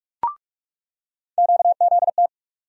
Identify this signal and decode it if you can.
Morse code, heard as a keyed tone, about 44 wpm, 700 Hz. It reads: QZT
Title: USA Call Sign Postfixes - 3 Letters 45wpm